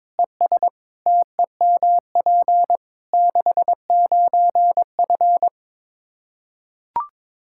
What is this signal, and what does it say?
Morse code, heard as a keyed tone, about 22 wpm, 700 Hz. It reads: ES TEMP 69F
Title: K1USN SST QSO Practice 22wpm